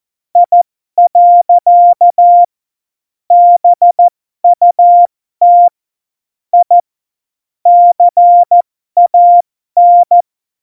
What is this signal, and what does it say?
Morse code, heard as a keyed tone, about 14 wpm, spaced as for 12 wpm, 700 Hz. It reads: I. BUT I CAN
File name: Morsekurs / Witze